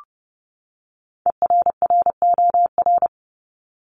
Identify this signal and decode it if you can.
Morse code, heard as a keyed tone, about 30 wpm, 700 Hz. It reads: ERROR
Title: Top 1751-2000 Words 30wpm